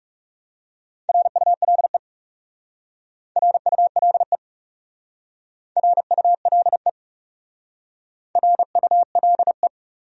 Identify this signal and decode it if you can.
Morse code, heard as a keyed tone, about 40 wpm, 700 Hz. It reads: RULE RULE RULE RULE